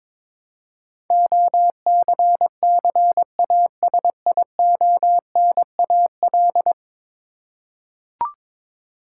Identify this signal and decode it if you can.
Morse code, heard as a keyed tone, about 22 wpm, 700 Hz. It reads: OCCASIONAL
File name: practice / 22wpm / 10 Letter Words - First 3 Letters Given - 22wpm